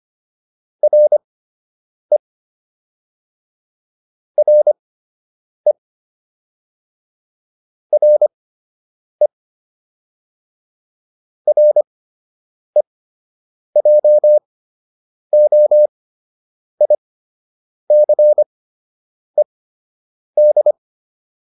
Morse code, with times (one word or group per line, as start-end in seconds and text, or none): 0.8-2.2 RE
4.4-5.7 RE
7.9-9.3 RE
11.5-20.7 REJOICED